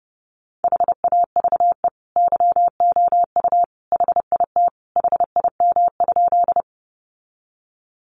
Morse code, heard as a keyed tone, about 30 wpm, 700 Hz. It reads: HAVE YOU HIT HIM?
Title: Sentences from Top 1200 Words 30wpm